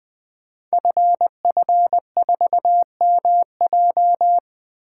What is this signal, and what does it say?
Morse code, heard as a keyed tone, about 20 wpm, 700 Hz. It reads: FF4MJ
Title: Code Groups - 5 Letter:Numbers - 20wpm